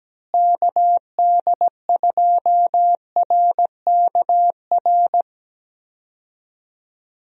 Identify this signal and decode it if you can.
Morse code, heard as a keyed tone, about 17 wpm, 700 Hz. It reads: KD2RKR
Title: USA Call Signs - 6 Characters - Repeated 2x - 17wpm